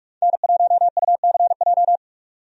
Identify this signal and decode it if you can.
Morse code, heard as a keyed tone, about 46 wpm, 700 Hz. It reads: N1UCJ
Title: USA Call Signs - 5 Characters - 45wpm